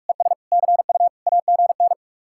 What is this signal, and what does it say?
Morse code, heard as a keyed tone, about 44 wpm, 700 Hz. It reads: ES CU AGN